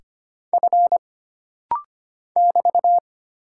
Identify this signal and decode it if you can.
Morse code, heard as a keyed tone, about 25 wpm, 700 Hz. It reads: F <BT>
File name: Single Character (Letter:Number:Punctuation:Prosign) - Rapid-Fire - 25wpm